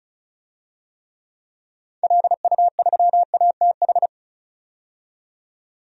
Morse code, heard as a keyed tone, about 34 wpm, 700 Hz. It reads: LU3ATH
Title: DXCC-100 Call Signs - 6 Characters - 35wpm